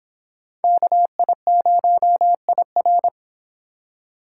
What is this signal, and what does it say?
Morse code, heard as a keyed tone, about 30 wpm, 700 Hz. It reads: KI0IR